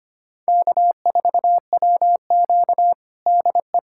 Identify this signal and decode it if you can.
Morse code, heard as a keyed tone, about 25 wpm, 700 Hz. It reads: K4WQ DE
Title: USA Callsign DE Callsign 25wpm